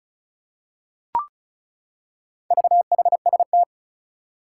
Morse code, heard as a keyed tone, about 36 wpm, 690 Hz. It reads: VHST
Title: Four Challenging Letter-Number Combinations 35wpm